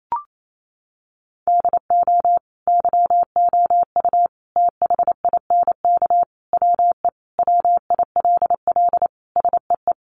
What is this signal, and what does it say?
Morse code, heard as a keyed tone, about 28 wpm, 700 Hz. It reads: DO YOU THINK WE WILL SEE
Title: Sentences from Top 1200 Words 28wpm